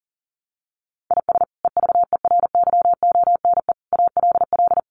Morse code, heard as a keyed tone, about 40 wpm, 700 Hz. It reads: IS EVERYONE ALL
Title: Sentences from Top 800 Words 40wpm